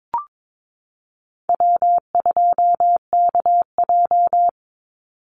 Morse code, heard as a keyed tone, about 22 wpm, 700 Hz. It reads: W2KJ
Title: Call Signs - 22wpm